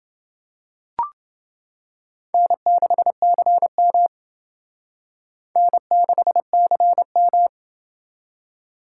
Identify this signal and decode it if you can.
Morse code, heard as a keyed tone, about 28 wpm, 700 Hz. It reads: N6CM N6CM